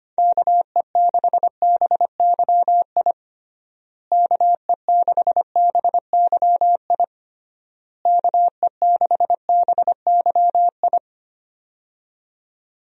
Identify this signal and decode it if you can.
Morse code, heard as a keyed tone, about 25 wpm, 700 Hz. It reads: KE6BYI KE6BYI KE6BYI
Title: USA Call Signs - 6 Characters - Repeated 3x - 25wpm